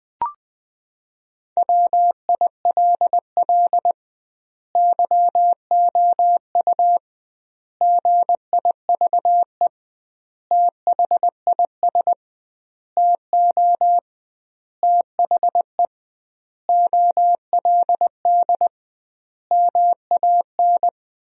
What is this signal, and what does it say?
Morse code, heard as a keyed tone, about 21 wpm, 700 Hz. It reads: WILL YOU GIVE THIS TO THE OLD MAN
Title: 2x Word-Spacing - Sentences from Top 200 Words - 20wpm